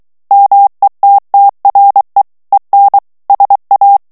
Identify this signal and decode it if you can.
Morse code, heard as a keyed tone, about 23 wpm, 800 Hz. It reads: METTRE EN SA
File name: morse mp3 / FRANCAIS TEXTE CLAIR 2 VITESSE 22 MOTS MINUTES